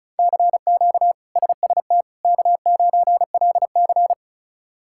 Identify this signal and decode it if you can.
Morse code, heard as a keyed tone, about 36 wpm, 700 Hz. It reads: CQ SST K9LC
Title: K1USN SST QSO Practice - 35wpm